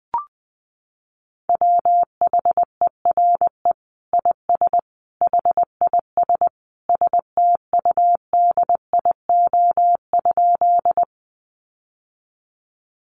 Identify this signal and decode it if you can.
Morse code, heard as a keyed tone, about 20 wpm, 700 Hz. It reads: WHERE IS HIS STUDIO?
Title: Sentences from Top 1200 Words 20wpm